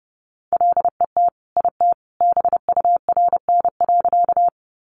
Morse code, heard as a keyed tone, about 30 wpm, 700 Hz. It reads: LET IT BURN.
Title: Sentences from Top 1300 Words 30wpm